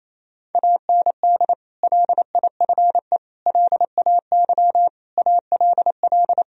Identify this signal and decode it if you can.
Morse code, heard as a keyed tone, about 28 wpm, 700 Hz. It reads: AND LIFE LAY ALL
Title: Sentences from Top 1000 Words - Review Part 2 - 28wpm